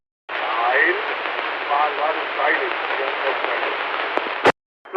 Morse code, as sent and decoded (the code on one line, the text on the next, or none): none